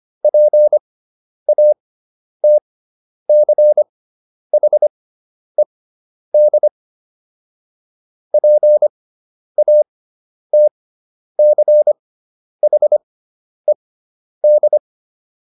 Morse code, heard as a keyed tone, about 25 wpm, spaced as for 10 wpm, 600 Hz. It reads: PATCHED PATCHED